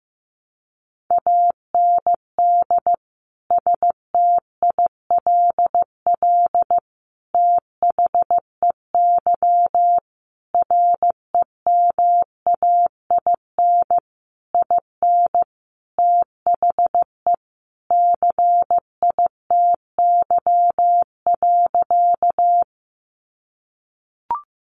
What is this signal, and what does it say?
Morse code, heard as a keyed tone, about 15 wpm, 700 Hz. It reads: AND STILL THEY REMAIN IN THE CITY.
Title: Sentences from Top 1200 Words 15wpm